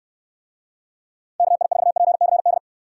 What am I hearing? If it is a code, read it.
Morse code, heard as a keyed tone, about 66 wpm, 700 Hz. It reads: KI4PCD